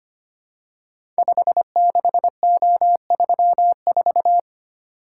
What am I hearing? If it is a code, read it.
Morse code, heard as a keyed tone, about 25 wpm, 700 Hz. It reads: 56O34
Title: Code Groups - 5 Letter:Numbers - 25wpm